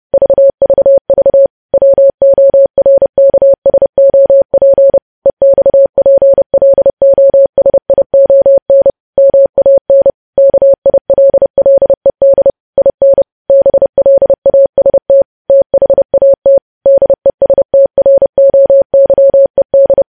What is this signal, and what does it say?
Morse code, heard as a keyed tone, about 30 wpm, 550 Hz. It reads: VVV WORKSOP EXPLOSION MAN KILLED IN BLAST THAT DESTROYED
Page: news headlines in morse code at 30 wpm